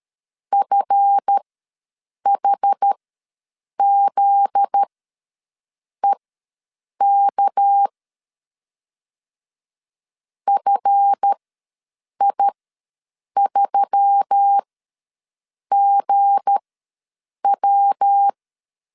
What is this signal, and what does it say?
Morse code, heard as a keyed tone, about 13 wpm, spaced as for 7 wpm, 800 Hz. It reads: FHZEK FI3GW
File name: 7 GF9Z ws 12wpm